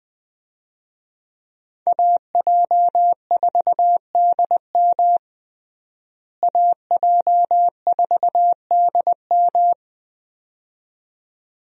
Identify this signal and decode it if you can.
Morse code, heard as a keyed tone, about 20 wpm, 700 Hz. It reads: AJ4DM AJ4DM
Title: USA Call Signs - 5 Characters - Repeated 2x - 20wpm